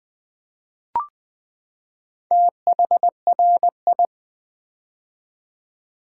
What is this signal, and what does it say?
Morse code, heard as a keyed tone, about 20 wpm, 700 Hz. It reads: THRI
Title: Build A Word - 4 Letters At A Time - 8-Letter Words - 20wpm